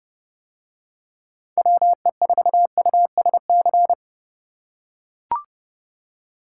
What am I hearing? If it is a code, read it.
Morse code, heard as a keyed tone, about 28 wpm, 700 Hz. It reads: WE4USC